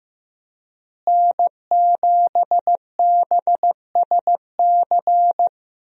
Morse code, heard as a keyed tone, about 16 wpm, 700 Hz. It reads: N7BSC
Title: USA Call Signs - 5 Characters - Repeated 2x - 15wpm